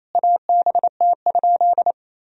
Morse code, heard as a keyed tone, about 28 wpm, 700 Hz. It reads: ABT?
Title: Sentences from Top 100 Words and Abbreviations 28wpm